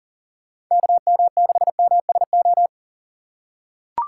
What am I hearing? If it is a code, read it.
Morse code, heard as a keyed tone, about 40 wpm, 700 Hz. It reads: KM6MSO
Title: USA Call Signs - 2x Character Spacing - 40wpm